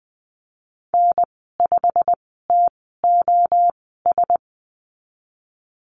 Morse code, heard as a keyed tone, about 20 wpm, spaced as for 15 wpm, 700 Hz. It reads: N5TOS
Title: 128 Lesson 21g - Number 6 - Call Signs 20wpm